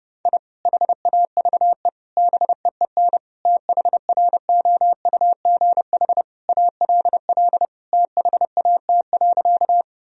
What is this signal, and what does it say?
Morse code, heard as a keyed tone, about 30 wpm, 700 Hz. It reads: I HAVE BEEN THROUGH ALL THAT.
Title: Sentences from Top 1000 Words - Review Part 1 - 30wpm